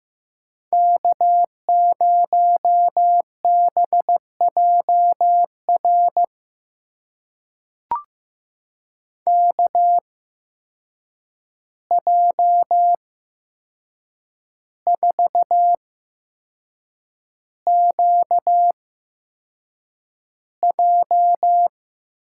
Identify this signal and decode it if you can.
Morse code, heard as a keyed tone, about 15 wpm, spaced as for 6 wpm, 700 Hz. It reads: K0BJR KJ4QJ
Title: USA Call Signs - 5 Characters - 8x Character Spacing - 15wpm